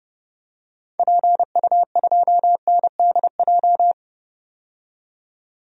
Morse code, heard as a keyed tone, about 30 wpm, 700 Hz. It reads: PU2NDJ
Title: DXCC-100 Call Signs - 6 Characters - 30wpm